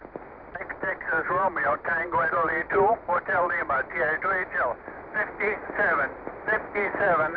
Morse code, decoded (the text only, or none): none